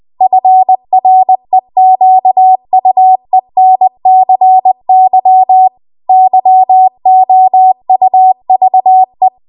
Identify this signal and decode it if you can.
Morse code, heard as a keyed tone, about 20 wpm, 750 Hz. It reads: FREQUENCY YOUVE
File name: Archive / 20 WPM / 20WPM